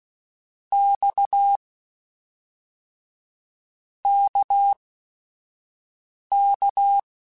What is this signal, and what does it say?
Morse code, heard as a keyed tone, about 16 wpm, 800 Hz. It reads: X K K
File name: Basic Sets / Basic Letter Groups Collective7 / RKPX@5